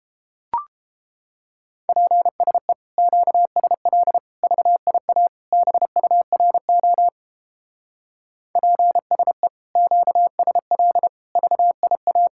Morse code, heard as a keyed tone, about 29 wpm, 700 Hz. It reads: PSE QSL VIA BURO PSE QSL VIA